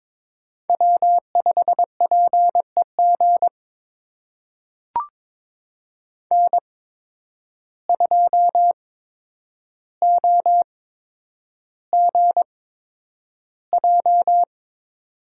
Morse code, heard as a keyed tone, about 22 wpm, spaced as for 9 wpm, 700 Hz. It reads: W5PEG N2OGJ
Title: USA Call Signs - 5 Characters - 8x Character Spacing - 22wpm